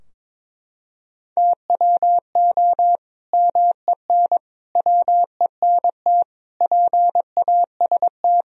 Morse code, heard as a keyed tone, about 22 wpm, 700 Hz. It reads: TWO MEN WENT PAST
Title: Sentences from Top 1000 Words - Review Part 3 - 22wpm